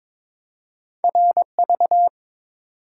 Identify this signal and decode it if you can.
Morse code, heard as a keyed tone, about 30 wpm, spaced as for 26 wpm, 700 Hz. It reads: RV
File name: Speed-Racing - Two Letter-Number Combinations - 22wpm